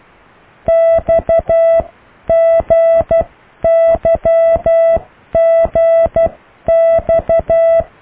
Morse code, heard as a keyed tone, about 12 wpm, spaced as for 10 wpm, 660 Hz. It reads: XGYGX